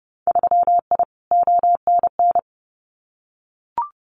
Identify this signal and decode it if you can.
Morse code, heard as a keyed tone, about 30 wpm, 700 Hz. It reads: 3I ONN
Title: Winter Field Day Exchanges Encore 30wpm